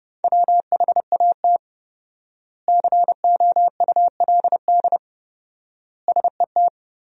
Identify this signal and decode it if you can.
Morse code, heard as a keyed tone, about 30 wpm, 700 Hz. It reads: WHAT COULD SET
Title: 4x Word-Spacing - Sentences from Top 200 Words - 30wpm